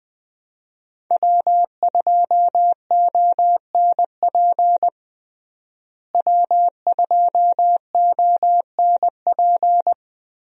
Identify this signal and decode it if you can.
Morse code, heard as a keyed tone, about 21 wpm, 700 Hz. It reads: W2ONP W2ONP
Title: USA Call Signs - 5 Characters - Repeated 3x - 20wpm